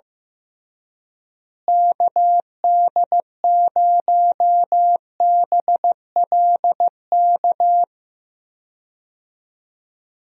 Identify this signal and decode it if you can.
Morse code, heard as a keyed tone, about 15 wpm, 700 Hz. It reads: KD0BLK
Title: USA Call Signs - 6 Characters - Repeated 3x - 15wpm